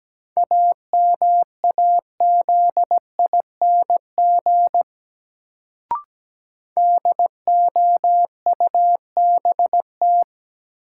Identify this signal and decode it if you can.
Morse code, heard as a keyed tone, about 17 wpm, 700 Hz. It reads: AMAZING DOUBT
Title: Sentences from Top 1500 Words - New Word Drill Alt - 17wpm